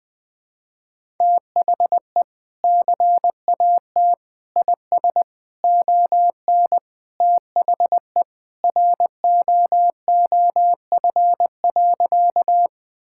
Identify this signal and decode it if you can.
Morse code, heard as a keyed tone, about 21 wpm, 700 Hz. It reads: THE CAT IS ON THE ROOF.
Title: Sentences with 3-Word Phrases - Spoken Beginning - 20wpm